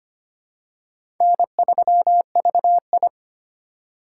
Morse code, heard as a keyed tone, about 25 wpm, 700 Hz. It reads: N3VI
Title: USA Call Signs - 4 Characters - 2x Character Spacing - 25wpm